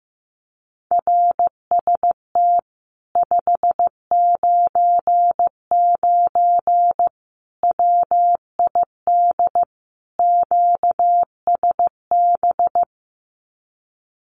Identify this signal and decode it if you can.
Morse code, heard as a keyed tone, about 15 wpm, 700 Hz. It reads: RST 599 WID QSB
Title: Common QSO Phrases Encore 15wpm